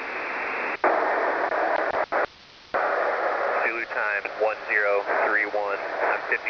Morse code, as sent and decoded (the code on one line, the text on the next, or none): none